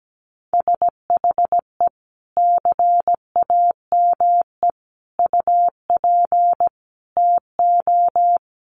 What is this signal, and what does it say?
Morse code, heard as a keyed tone, about 17 wpm, 700 Hz. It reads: SHE CAME UP TO
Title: Sentences from Top 200 Words 17wpm